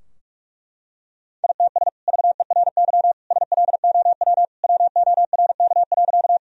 Morse code, heard as a keyed tone, about 47 wpm, 700 Hz. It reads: ITS VERY SLOW WORK.
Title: Sentences from Top 1000 Words - Review Part 2 - 45wpm